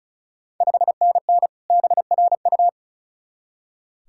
Morse code, heard as a keyed tone, about 36 wpm, 700 Hz. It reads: 5NN BRU
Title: Ontario QSO Party Exchange 35wpm